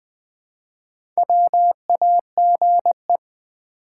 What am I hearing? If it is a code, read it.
Morse code, heard as a keyed tone, about 20 wpm, 700 Hz. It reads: WAGE
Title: Build A Word - 1 Letter At A Time - 3 to 5-Letter Words - 20wpm